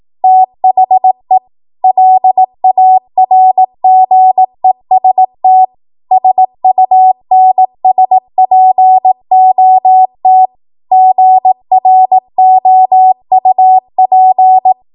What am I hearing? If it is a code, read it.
Morse code, heard as a keyed tone, about 18 wpm, 750 Hz. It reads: THE LARGEST SUNSPOT GROUP
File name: Morse / 18WPM